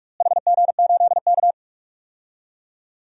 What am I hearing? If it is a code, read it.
Morse code, heard as a keyed tone, about 47 wpm, 700 Hz. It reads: HG8K
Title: DXCC-100 Call Signs - 4 Characters - Repeated 3x - 45wpm